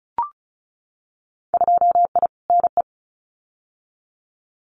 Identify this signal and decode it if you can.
Morse code, heard as a keyed tone, about 36 wpm, 700 Hz. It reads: 2I NE
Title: Winter Field Day Exchanges Encore 35wpm